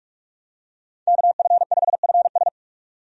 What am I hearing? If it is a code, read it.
Morse code, heard as a keyed tone, about 46 wpm, 700 Hz. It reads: KF5FS
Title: USA Call Signs - 5 Characters - 45wpm